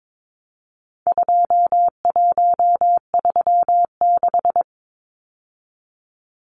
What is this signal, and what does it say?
Morse code, heard as a keyed tone, about 22 wpm, 700 Hz. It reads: 2136
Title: Numbers - 4-Digit 22wpm